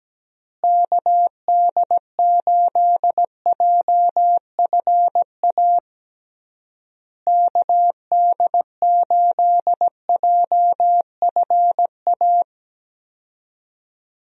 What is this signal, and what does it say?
Morse code, heard as a keyed tone, about 17 wpm, 700 Hz. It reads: KD8JFA KD8JFA
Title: USA Call Signs - 6 Characters - Repeated 3x - 17wpm